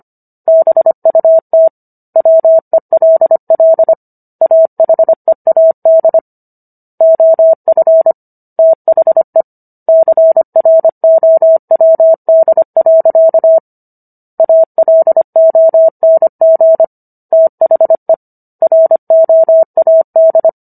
Morse code, heard as a keyed tone, about 25 wpm, 650 Hz. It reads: BUT WELL AHEAD OF THE CROWD. ALONG THE ROAD